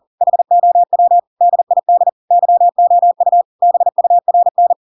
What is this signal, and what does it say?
Morse code, heard as a keyed tone, about 40 wpm, 700 Hz. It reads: HOW DID YOU BURN
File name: Pinocchio / Pinocchio ch7 40 40 ik1hge